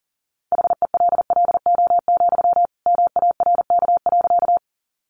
Morse code, heard as a keyed tone, about 40 wpm, 700 Hz. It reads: HELLO, MARK.
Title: Sentences from Top 1100 Words 40wpm